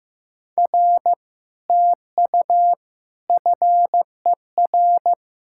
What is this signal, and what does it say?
Morse code, heard as a keyed tone, about 15 wpm, 700 Hz. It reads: R TU FER
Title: Speed-Racing Common QSO Phrases Encore - 15wpm